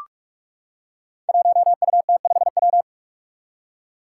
Morse code, heard as a keyed tone, about 44 wpm, 700 Hz. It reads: 1UT5W
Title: Five Challenging Letter-Number Combinations 45wpm